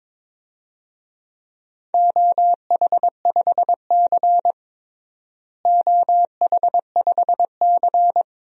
Morse code, heard as a keyed tone, about 22 wpm, 700 Hz. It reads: OH5C OH5C
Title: DXCC-100 Call Signs - 4 Characters - Repeated 2x - 22wpm